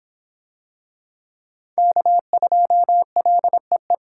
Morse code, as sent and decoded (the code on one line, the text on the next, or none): -.- ..--- .-.. .
K2LEE